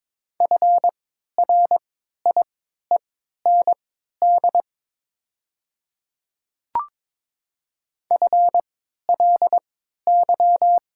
Morse code, heard as a keyed tone, about 22 wpm, spaced as for 12 wpm, 700 Hz. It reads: FRIEND FLY
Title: 3x Character Spacing - Top 500 Words - 22wpm